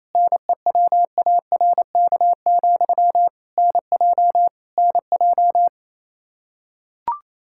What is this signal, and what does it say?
Morse code, heard as a keyed tone, about 28 wpm, 700 Hz. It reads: NEWARK, NJ NJ
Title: K1USN SST QSO Practice 28wpm